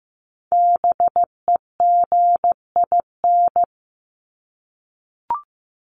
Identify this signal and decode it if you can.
Morse code, heard as a keyed tone, about 15 wpm, 700 Hz. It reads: BEGIN
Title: Top 500 Words Repeated 15wpm